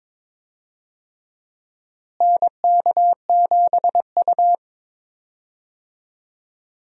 Morse code, heard as a keyed tone, about 22 wpm, 700 Hz. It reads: NK7U
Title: USA Call Signs - 4 Characters - Repeated 3x - 22wpm